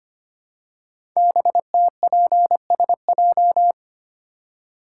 Morse code, heard as a keyed tone, about 25 wpm, 700 Hz. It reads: BTPSJ
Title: Code Groups - 5 Letters - 25wpm